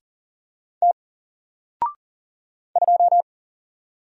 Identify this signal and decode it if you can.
Morse code, heard as a keyed tone, about 40 wpm, 700 Hz. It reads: T 2
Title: Single Letter-Number 40wpm